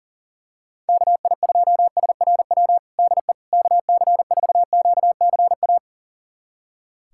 Morse code, heard as a keyed tone, about 40 wpm, 700 Hz. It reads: KI2SRW DE KC4QCA
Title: USA Callsign DE Callsign 40wpm